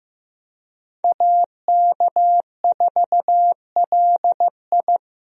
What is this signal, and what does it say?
Morse code, heard as a keyed tone, about 15 wpm, 700 Hz. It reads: AK4LI
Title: USA Call Signs - 5 Characters - Repeated 3x - 15wpm